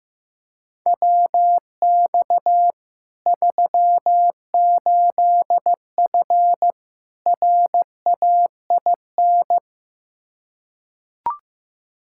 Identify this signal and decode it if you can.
Morse code, heard as a keyed tone, about 15 wpm, 700 Hz. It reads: WX 38F RAIN
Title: Practice QSO - Exchange 2 - Info - 15wpm